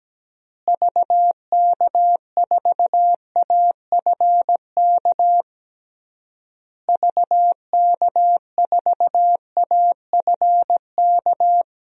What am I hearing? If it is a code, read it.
Morse code, heard as a keyed tone, about 17 wpm, 700 Hz. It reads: VK4AFK VK4AFK
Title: DXCC-100 Call Signs - 6 Characters - Repeated 3x - 17wpm